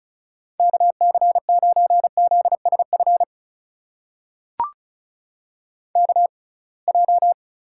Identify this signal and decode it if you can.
Morse code, heard as a keyed tone, about 36 wpm, spaced as for 27 wpm, 700 Hz. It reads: KC9ZSF KJ
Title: USA Call Signs - 6 Characters - 6x Character Spacing - 35wpm